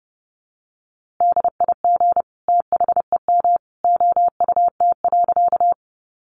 Morse code, heard as a keyed tone, about 30 wpm, 700 Hz. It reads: DIG THEM OUT.
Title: A Sound Of Thunder 30wpm